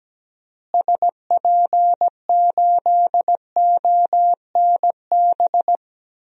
Morse code, heard as a keyed tone, about 17 wpm, 700 Hz. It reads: SP8ONB
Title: DXCC-100 Call Signs - 6 Characters - Repeated 3x - 17wpm